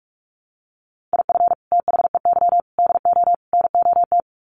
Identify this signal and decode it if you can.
Morse code, heard as a keyed tone, about 44 wpm, 700 Hz. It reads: IF THEY DO NOT
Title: Sets of 4 Words 45wpm